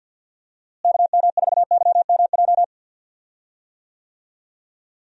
Morse code, heard as a keyed tone, about 52 wpm, 700 Hz. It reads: KM4YMJ
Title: USA Call Signs - 6 Characters - Repeated 2x - 50wpm